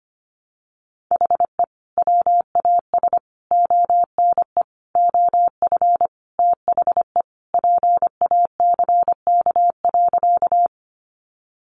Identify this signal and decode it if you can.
Morse code, heard as a keyed tone, about 25 wpm, 700 Hz. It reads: HE WAS ONE OF THE PACK.
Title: Sentences from Top 1300 Words 25wpm